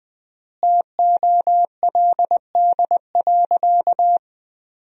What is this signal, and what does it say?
Morse code, heard as a keyed tone, about 20 wpm, 700 Hz. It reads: TOLD.
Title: 3x Word-Spacing - Sentences from Top 400 Words - 20wpm